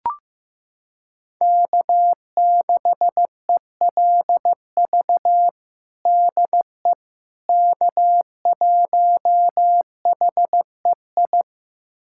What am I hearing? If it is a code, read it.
Morse code, heard as a keyed tone, about 15 wpm, 700 Hz. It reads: K6ELV DE K1HEI
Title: USA Callsign DE Callsign - 5 Characters - 15wpm